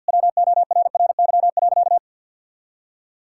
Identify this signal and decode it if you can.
Morse code, heard as a keyed tone, about 50 wpm, 700 Hz. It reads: WORRY.
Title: Pangram Challenge v2 - 50wpm